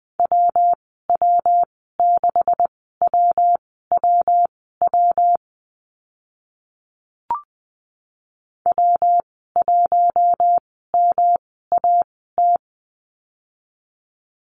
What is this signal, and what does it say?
Morse code, heard as a keyed tone, about 20 wpm, spaced as for 14 wpm, 700 Hz. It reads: WW6WWW W1MAT
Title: 140 Lesson 23g - Letter f - Call Signs 20wpm